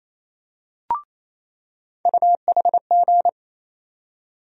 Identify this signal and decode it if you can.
Morse code, heard as a keyed tone, about 28 wpm, 700 Hz. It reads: UHG
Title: USA Call Sign Postfixes - 3 Letters 28wpm